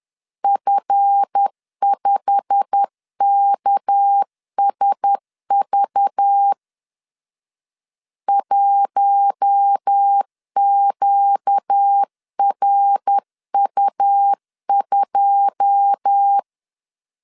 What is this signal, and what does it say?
Morse code, heard as a keyed tone, about 10 wpm, 800 Hz. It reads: F5KSV 1QRU2
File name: CW / DET / 1 9 1500C S B 10wpm 4439